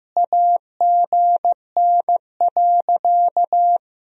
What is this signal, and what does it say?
Morse code, heard as a keyed tone, about 16 wpm, 700 Hz. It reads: AGN.
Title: Sentences from Top 500 Words and Abbreviations 15wpm